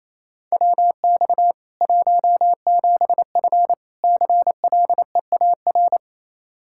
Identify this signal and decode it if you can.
Morse code, heard as a keyed tone, about 28 wpm, 700 Hz. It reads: WX 17F CLEAR
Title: Practice QSO - Exchange 2 - Info - 28wpm